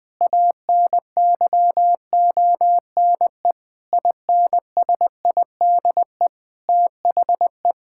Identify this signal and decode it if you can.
Morse code, heard as a keyed tone, about 20 wpm, 700 Hz. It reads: ANYONE INSIDE THE
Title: Sentences from Top 1000 Words - Review Part 6 - 20wpm